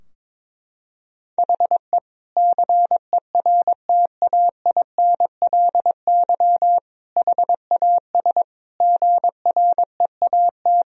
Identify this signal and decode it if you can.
Morse code, heard as a keyed tone, about 22 wpm, 700 Hz. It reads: HE CERTAINLY HAS GREAT
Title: Sentences from Top 1000 Words - Review Part 3 - 22wpm